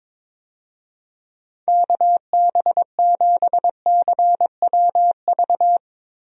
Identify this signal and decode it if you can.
Morse code, heard as a keyed tone, about 22 wpm, 700 Hz. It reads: KB7CWV
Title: USA Call Signs - 6 Characters - 22wpm